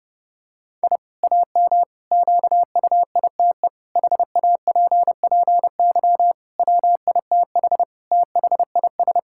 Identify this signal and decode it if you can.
Morse code, heard as a keyed tone, about 32 wpm, 700 Hz. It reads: I AM QUITE HAPPY WITH THIS
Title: Sentences from Top 1000 Words - Review Part 5 - 30wpm